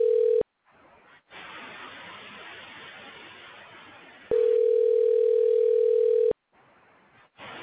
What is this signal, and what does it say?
Ringback tone (North America), 440 + 480 Hz.
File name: recordings / REf4b2f6655b6cc69b89741fd7b4194ab7